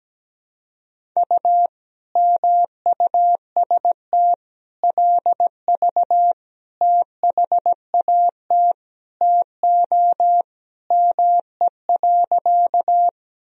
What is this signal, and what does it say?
Morse code, heard as a keyed tone, about 17 wpm, 700 Hz. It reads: U MUST LV THAT TO ME.
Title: Sentences from Top 400 Words and Abbreviations 17wpm